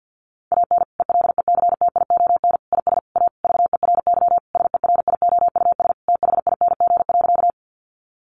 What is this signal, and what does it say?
Morse code, heard as a keyed tone, about 49 wpm, 700 Hz. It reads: AN ELECTION IS A VERY SERIOUS THING.
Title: Sentences from Top 1500 Words 50wpm